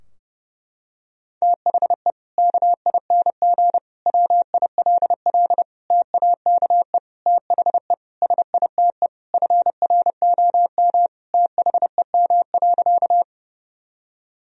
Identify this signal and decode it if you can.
Morse code, heard as a keyed tone, about 30 wpm, 700 Hz. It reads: THE KING WILL TAKE THE SITE FROM THEM.
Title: Sentences from Top 1000 Words - Review Part 4 - 30wpm